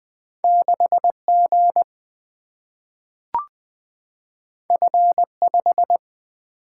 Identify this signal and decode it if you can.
Morse code, heard as a keyed tone, about 20 wpm, 700 Hz. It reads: 6G F5
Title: Single-Letter Single-Number 20wpm